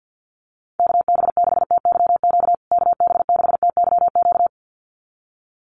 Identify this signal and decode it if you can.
Morse code, heard as a keyed tone, about 48 wpm, 700 Hz. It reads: KB6TYQ KB6TYQ
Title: USA Call Signs Repeated 2x 50wpm